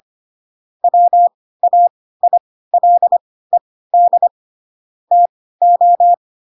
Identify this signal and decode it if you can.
Morse code, heard as a keyed tone, about 25 wpm, spaced as for 16 wpm, 700 Hz. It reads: WAILED TO
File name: Pinocchio / Pinocchio ch5 10 25 ik1hge